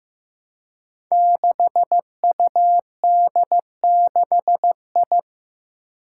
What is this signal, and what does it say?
Morse code, heard as a keyed tone, about 15 wpm, 700 Hz. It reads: 6UD6I
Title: Five Challenging Letter-Number Combinations 15wpm